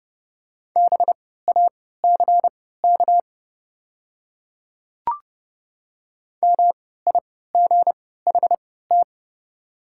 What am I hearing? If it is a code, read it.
Morse code, heard as a keyed tone, about 30 wpm, spaced as for 17 wpm, 700 Hz. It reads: BACK MIGHT
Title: 3x Character Spacing - Top 500 Words - 30wpm